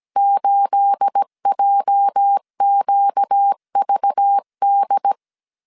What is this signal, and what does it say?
Morse code, heard as a keyed tone, about 17 wpm, 800 Hz. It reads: 8JQVD
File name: CW / DET / 1 9 1500C S C 16wpm 2909